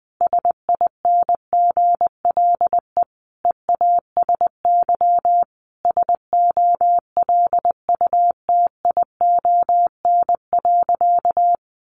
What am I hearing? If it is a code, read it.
Morse code, heard as a keyed tone, about 20 wpm, 700 Hz. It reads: SINGLE EASY SOLUTION.
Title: Sentences from Top 1300 Words 20wpm